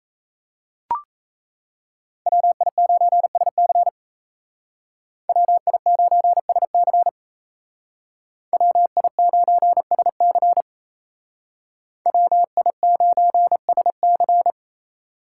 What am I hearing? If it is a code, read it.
Morse code, heard as a keyed tone, about 33 wpm, 700 Hz. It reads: WI9SC WI9SC WI9SC WI9SC
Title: Speed Racing - USA Call Signs 28wpm